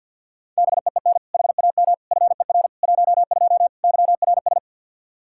Morse code, heard as a keyed tone, about 51 wpm, 700 Hz. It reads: BEEN HAM FER 12 YRS
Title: Practice QSO - Exchange 3 - More Info - 50wpm